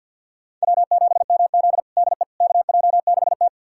Morse code, heard as a keyed tone, about 49 wpm, 700 Hz. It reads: W7MZ DE KJ6T
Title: USA Callsign DE Callsign - 4 Characters - 50wpm